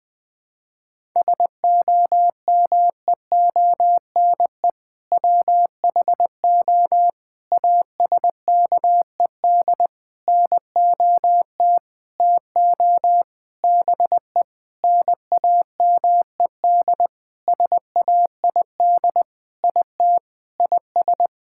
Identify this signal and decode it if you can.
Morse code, heard as a keyed tone, about 20 wpm, 700 Hz. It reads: SOMEONE WHO ASKED NOT TO BE NAMED SAID IT IS
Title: Sentences with 7-Word Phrases - 20wpm